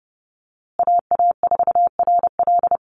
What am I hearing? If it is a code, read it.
Morse code, heard as a keyed tone, about 30 wpm, 700 Hz. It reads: AA4RL
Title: USA Call Signs Repeated 2x 30wpm